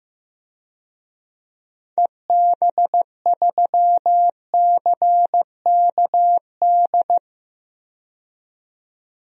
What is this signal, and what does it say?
Morse code, heard as a keyed tone, about 15 wpm, 700 Hz. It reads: EB3CKD
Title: DXCC-100 Call Signs - 6 Characters - 15wpm